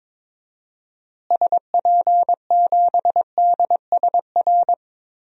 Morse code, heard as a keyed tone, about 21 wpm, 700 Hz. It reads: SP7DSR
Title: DXCC-100 Call Signs - 6 Characters - Repeated 3x - 22wpm